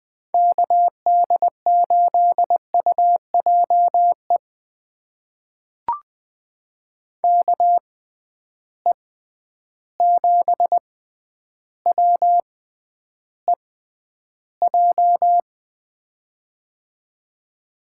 Morse code, heard as a keyed tone, about 20 wpm, spaced as for 10 wpm, 700 Hz. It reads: KD8UJE KE7WEJ
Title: USA Call Signs - 6 Characters - 6x Character Spacing - 20wpm